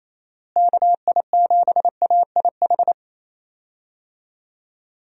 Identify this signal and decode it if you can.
Morse code, heard as a keyed tone, about 28 wpm, 700 Hz. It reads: KI7AIH
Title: USA Call Signs - 6 Characters - Repeated 3x - 28wpm